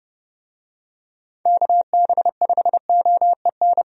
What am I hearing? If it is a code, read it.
Morse code, heard as a keyed tone, about 30 wpm, 700 Hz. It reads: KB5OEN
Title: USA Call Signs - 6 Characters - 30wpm